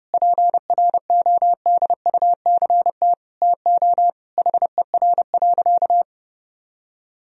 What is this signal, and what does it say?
Morse code, heard as a keyed tone, about 32 wpm, 700 Hz. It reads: PRODUCT TO HER.
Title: Sentences from Top 1000 Words - Review Part 3 - 30wpm